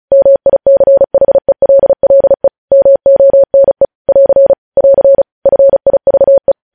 Morse code, heard as a keyed tone, about 36 wpm, 550 Hz. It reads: MICHELLE MONE <AR> <AR> FIVE
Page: News Headlines in Morse Code at 35 WPM